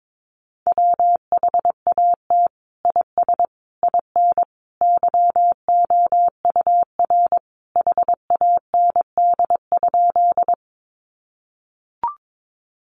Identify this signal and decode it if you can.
Morse code, heard as a keyed tone, about 22 wpm, 700 Hz. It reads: WHAT IS IN YOUR HAND?
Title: Sentences from Top 200 Words 22wpm